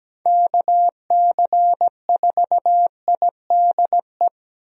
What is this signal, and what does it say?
Morse code, heard as a keyed tone, about 17 wpm, 700 Hz. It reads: KC4IDE
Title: USA Call Signs - 6 Characters - Repeated 2x - 17wpm